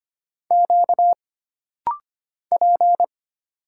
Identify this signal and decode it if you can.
Morse code, heard as a keyed tone, about 25 wpm, 700 Hz. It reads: Q P
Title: Single Letters - Rapid-Fire 25wpm